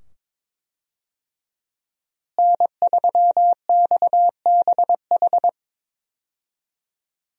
Morse code, heard as a keyed tone, about 22 wpm, 700 Hz. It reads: N3XBH
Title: USA Call Signs - 5 Characters - Repeated 2x - 22wpm